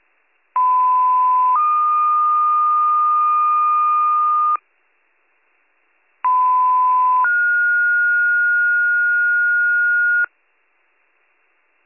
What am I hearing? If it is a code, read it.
Two-tone page, heard as a one-second tone, then a three-second tone, about 1010 then 1220 Hz, about 1010 then 1430 Hz. It is heard from 0.6 to 10.2 s.